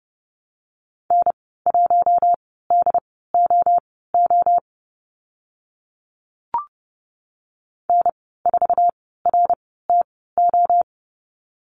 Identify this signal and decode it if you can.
Morse code, heard as a keyed tone, about 30 wpm, spaced as for 17 wpm, 690 Hz. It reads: N1DOO N4RTO
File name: Lesson 15g - Number 5 - Call Signs 30wpm